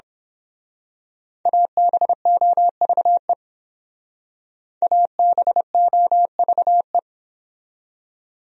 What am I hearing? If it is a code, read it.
Morse code, heard as a keyed tone, about 26 wpm, 700 Hz. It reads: ABOVE ABOVE
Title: Speed-Racing Top 500 Words 22wpm